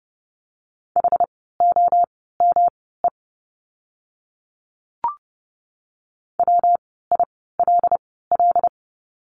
Morse code, heard as a keyed tone, about 30 wpm, spaced as for 17 wpm, 700 Hz. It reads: HOME WILL
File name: course / 30:10wpm / 115 Lesson 19f - Letter w - Words 30wpm